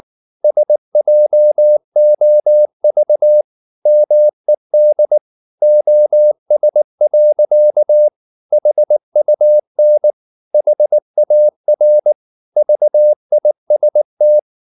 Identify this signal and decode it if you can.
Morse code, heard as a keyed tone, about 19 wpm, 600 Hz. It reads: SJOV MED OS. HUN HAR VIST
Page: Ferie 19 wpm